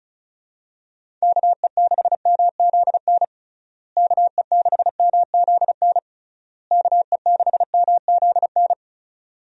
Morse code, heard as a keyed tone, about 36 wpm, 700 Hz. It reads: KE6MZN KE6MZN KE6MZN
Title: USA Call Signs - 6 Characters - Repeated 3x - 35wpm